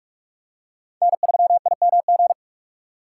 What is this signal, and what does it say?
Morse code, heard as a keyed tone, about 39 wpm, 700 Hz. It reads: N3IMG